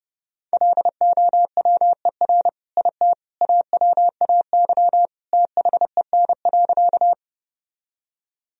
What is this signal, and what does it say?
Morse code, heard as a keyed tone, about 30 wpm, 700 Hz. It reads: LOWER IT AWAY THEN.
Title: Sentences from Top 1000 Words - Review Part 4 - 30wpm